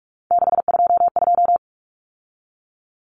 Morse code, heard as a keyed tone, about 44 wpm, 700 Hz. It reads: 622
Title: Common Cut Numbers - 3-Digit 45wpm